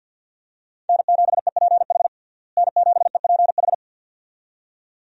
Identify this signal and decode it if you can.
Morse code, heard as a keyed tone, about 52 wpm, 700 Hz. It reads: N7EPH N7EPH
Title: USA Call Signs - 5 Characters - Repeated 2x - 50wpm